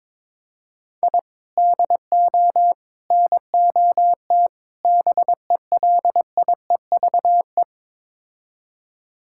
Morse code, heard as a keyed tone, about 21 wpm, 700 Hz. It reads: I DO NOT BELIEVE
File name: practice / 22wpm / Build A Phrase - 5-Word Phrases - 22wpm